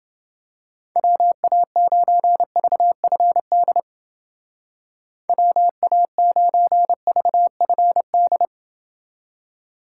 Morse code, heard as a keyed tone, about 29 wpm, 700 Hz. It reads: WA9VFD WA9VFD